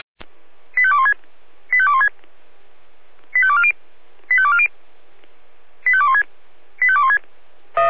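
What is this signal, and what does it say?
Five-tone selective call (ZVEI), heard as quick runs of five tones, tones of about 70 ms. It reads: 85216 85216 85240 85240 85216 85216
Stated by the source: ZVEI five-tone selective call received on an SDR.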